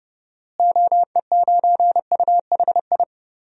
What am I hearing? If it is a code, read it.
Morse code, heard as a keyed tone, about 32 wpm, 700 Hz. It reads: OE9UHI
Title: DXCC-100 Call Signs - 6 Characters - Repeated 2x - 30wpm